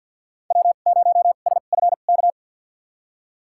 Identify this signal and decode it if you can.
Morse code, heard as a keyed tone, about 51 wpm, spaced as for 36 wpm, 700 Hz. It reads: W0SFK
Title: USA Call Signs - 5 Characters - 2x Character Spacing - 50wpm